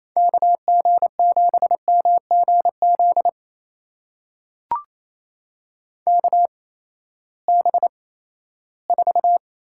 Morse code, heard as a keyed tone, about 28 wpm, spaced as for 16 wpm, 700 Hz. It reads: KG7MGZ KB4
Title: USA Call Signs - 6 Characters - 8x Character Spacing - 28wpm